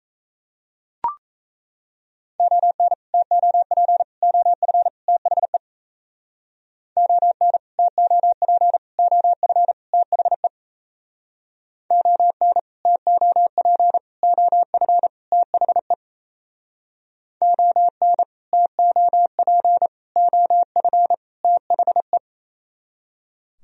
Morse code, heard as a keyed tone, about 37 wpm, 700 Hz. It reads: ON TOP OF THE ON TOP OF THE ON TOP OF THE ON TOP OF THE